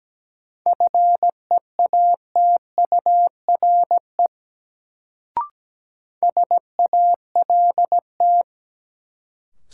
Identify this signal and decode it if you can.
Morse code, heard as a keyed tone, about 17 wpm, 700 Hz. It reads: FEATURE SALT
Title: Sentences from Top 800 Words - New Word Drill - 17wpm